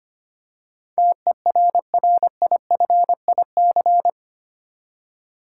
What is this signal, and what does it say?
Morse code, heard as a keyed tone, about 25 wpm, 700 Hz. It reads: TERRIFIC
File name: Eight Letter Words - 25wpm